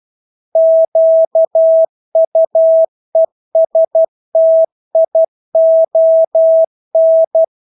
Morse code, heard as a keyed tone, about 12 wpm, 650 Hz. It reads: QUESTION
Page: Engelske ord 12 wpm